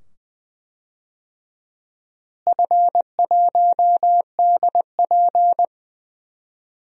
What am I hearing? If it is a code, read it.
Morse code, heard as a keyed tone, about 20 wpm, 700 Hz. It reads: F1DP
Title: DXCC-100 Call Signs - 4 Characters - Repeated 2x - 20wpm